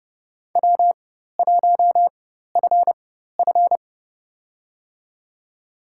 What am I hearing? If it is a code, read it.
Morse code, heard as a keyed tone, about 30 wpm, spaced as for 14 wpm, 700 Hz. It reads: W1FF
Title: USA Call Signs - 4x Character Spacing - 30wpm